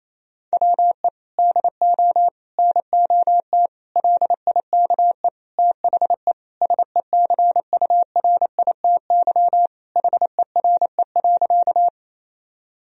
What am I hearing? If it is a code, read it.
Morse code, heard as a keyed tone, about 28 wpm, 700 Hz. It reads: WE DO NOT LIKE THE SECURITY HERE.
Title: Sentences from Top 1000 Words - Review Part 4 - 28wpm